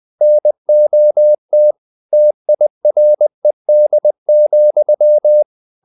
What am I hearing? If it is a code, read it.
Morse code, heard as a keyed tone, about 20 wpm, 600 Hz. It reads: NOT TIRED,
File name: the old man and the sea / the old man and the sea part 1 20wpm